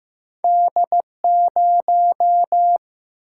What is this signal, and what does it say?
Morse code, heard as a keyed tone, about 15 wpm, 700 Hz. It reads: D0
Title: Single-Letter Single-Number 15wpm